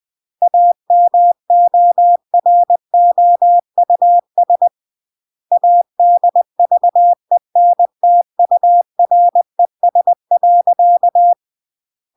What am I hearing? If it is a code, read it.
Morse code, heard as a keyed tone, about 20 wpm, 700 Hz. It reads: AMOROUS ADVENTURES.
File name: CW / jodorowsky / 82 commandments 20-20